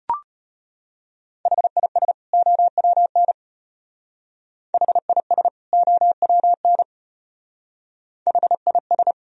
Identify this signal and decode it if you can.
Morse code, heard as a keyed tone, about 32 wpm, 700 Hz. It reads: HIS OWN HIS OWN HIS